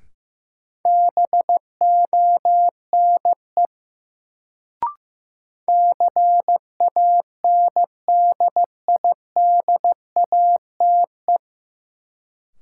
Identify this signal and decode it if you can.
Morse code, heard as a keyed tone, about 15 wpm, 700 Hz. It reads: BONE CANDIDATE
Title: Sentences from Top 1600 Words - New Word Drill - 15wpm